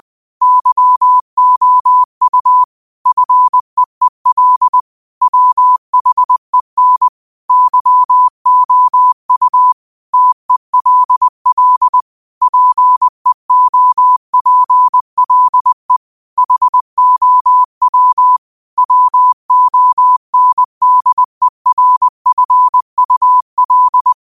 Morse code, heard as a keyed tone, about 20 wpm, 1000 Hz. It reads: YOU FEEL WHEN YOU TELL PEOPLE HOW WONDERFUL
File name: qotdfiles / 20wpm